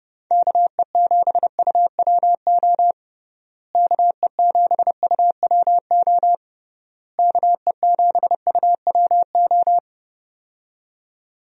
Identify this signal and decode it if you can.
Morse code, heard as a keyed tone, about 31 wpm, 700 Hz. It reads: KE7UWO KE7UWO KE7UWO
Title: USA Call Signs - 6 Characters - Repeated 3x - 30wpm